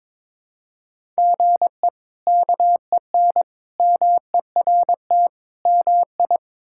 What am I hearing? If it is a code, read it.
Morse code, heard as a keyed tone, about 22 wpm, 700 Hz. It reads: GE KEN MERT MI
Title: K1USN SST QSO Practice - 22wpm